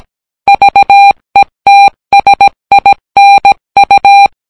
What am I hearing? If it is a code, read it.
Morse code, heard as a keyed tone, about 17 wpm, 800 Hz. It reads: VETSINU